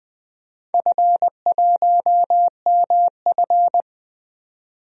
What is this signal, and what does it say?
Morse code, heard as a keyed tone, about 20 wpm, 700 Hz. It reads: F1MF
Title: DXCC-100 Call Signs - 4 Characters - 20wpm